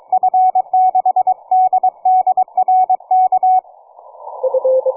Morse code, heard as a keyed tone, about 23 wpm, 740 Hz. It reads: F6DDR K
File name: cw forum / 9m2ax 160m1829 22h14